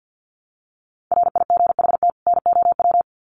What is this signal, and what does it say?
Morse code, heard as a keyed tone, about 50 wpm, 700 Hz. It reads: RIGHT NOW
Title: Sets of 2 Words 50wpm